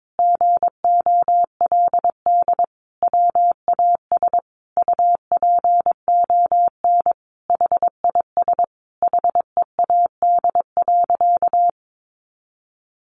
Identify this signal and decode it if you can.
Morse code, heard as a keyed tone, about 22 wpm, 700 Hz. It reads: GOLD WAS UPON HIS HEAD.
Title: Sentences from Top 1100 Words 22wpm